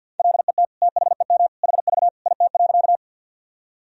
Morse code, heard as a keyed tone, about 47 wpm, 700 Hz. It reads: LET THEM HV IT.
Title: Sentences from Top 300 Words and Abbreviations 50wpm